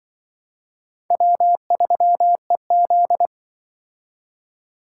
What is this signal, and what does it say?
Morse code, heard as a keyed tone, about 24 wpm, 700 Hz. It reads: W3EZ
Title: Speed Racing - USA Call Signs 20wpm